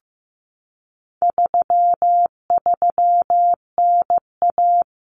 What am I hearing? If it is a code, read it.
Morse code, heard as a keyed tone, about 15 wpm, 700 Hz. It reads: 33NA
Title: Common Cut Numbers - 4-Digit 15wpm